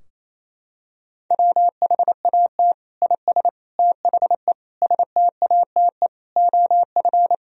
Morse code, heard as a keyed tone, about 28 wpm, 700 Hz. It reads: WHAT IS THE STATE OF
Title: Sentences from Top 1000 Words - Review Part 3 - 28wpm